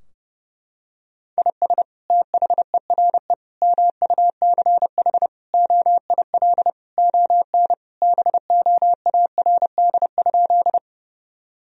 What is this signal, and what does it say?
Morse code, heard as a keyed tone, about 30 wpm, 700 Hz. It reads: IS THERE MUCH OIL ON BOARD?
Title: Sentences from Top 1000 Words - Review Part 4 - 30wpm